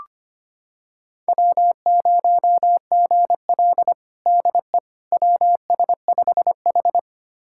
Morse code, heard as a keyed tone, about 25 wpm, 700 Hz. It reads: W0GL DE WS5H
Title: USA Callsign DE Callsign - 4 Characters - 25wpm